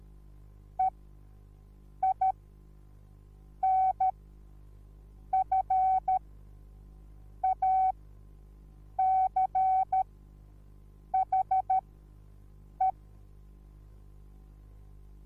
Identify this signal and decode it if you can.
Morse code, heard as a keyed tone, about 13 wpm, spaced as for 6 wpm, 760 Hz. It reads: EINFACHE
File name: Onlinelehrgang / morsen / DJ4UF Morsen21